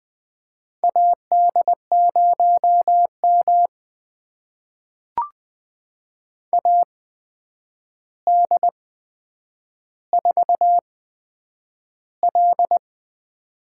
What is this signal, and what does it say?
Morse code, heard as a keyed tone, about 20 wpm, spaced as for 9 wpm, 700 Hz. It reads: AD0M AD4L
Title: USA Call Signs - 4 Characters - 8x Character Spacing - 20wpm